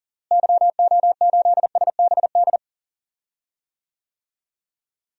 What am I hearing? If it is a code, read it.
Morse code, heard as a keyed tone, about 40 wpm, 700 Hz. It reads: YO8SBD
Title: DXCC-100 Call Signs - 6 Characters - Repeated 3x - 40wpm